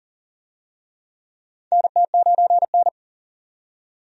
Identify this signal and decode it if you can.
Morse code, heard as a keyed tone, about 40 wpm, 700 Hz. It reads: NT9N
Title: USA Call Signs - 4 Characters - 40wpm